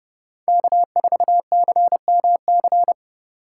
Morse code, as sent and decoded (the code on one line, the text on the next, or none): -.- ....- -.-. -- -.-.
K4CMC